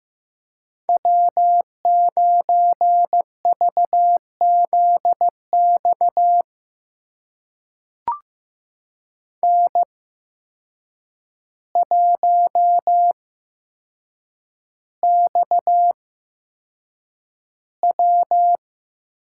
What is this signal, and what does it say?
Morse code, heard as a keyed tone, about 15 wpm, spaced as for 7 wpm, 700 Hz. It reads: W9VZX N1XW
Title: USA Call Signs - 8x Character Spacing - 15wpm